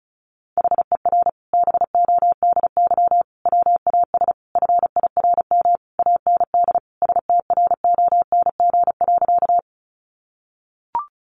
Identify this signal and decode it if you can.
Morse code, heard as a keyed tone, about 34 wpm, 700 Hz. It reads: HER BODY WAS FIRM AND STRONG.
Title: Sentences from Top 1400 Words 35wpm